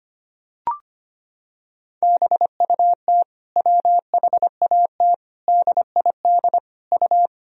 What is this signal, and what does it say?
Morse code, heard as a keyed tone, about 25 wpm, 700 Hz. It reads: BUT WHAT DID U
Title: Sentences from Top 200 Words and Abbreviations 25wpm